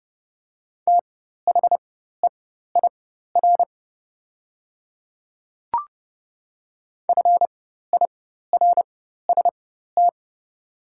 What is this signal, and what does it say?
Morse code, heard as a keyed tone, about 30 wpm, spaced as for 14 wpm, 700 Hz. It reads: THEIR FIRST
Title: 4x Character Spacing - Top 100 Words - 30wpm